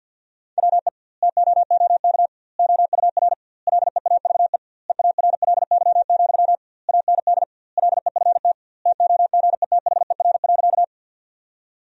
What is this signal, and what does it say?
Morse code, heard as a keyed tone, about 50 wpm, 700 Hz. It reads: WE TOOK OUR LEAVE EARLY, AND LEFT TOGETHER.
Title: 2x Word-Spacing - Sentences from Top 400 Words - 50wpm